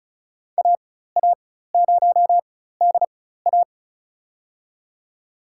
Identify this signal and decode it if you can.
Morse code, heard as a keyed tone, about 36 wpm, spaced as for 16 wpm, 700 Hz. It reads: AA0DA
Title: USA Call Signs - 5 Characters - 4x Character Spacing - 35wpm